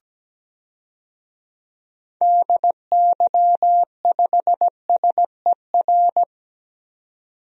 Morse code, heard as a keyed tone, about 17 wpm, 700 Hz. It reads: DY5SER
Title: DXCC-100 Call Signs - 6 Characters - 17wpm